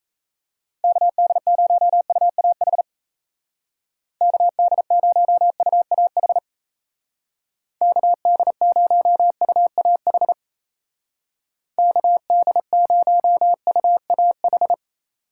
Morse code, heard as a keyed tone, about 35 wpm, 700 Hz. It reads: KD0UAH KD0UAH KD0UAH KD0UAH
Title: Speed Racing - USA Call Signs 28wpm